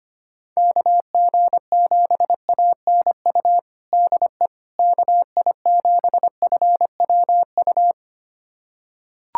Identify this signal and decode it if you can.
Morse code, heard as a keyed tone, about 25 wpm, 700 Hz. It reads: KG7ANU DE KI7FWU
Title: USA Callsign DE Callsign - 6 Characters - 25wpm